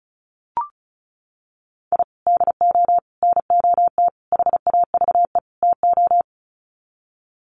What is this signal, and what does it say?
Morse code, heard as a keyed tone, about 34 wpm, 700 Hz. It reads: I DO NOT HAVE TO
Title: Sets of 5 Words 35wpm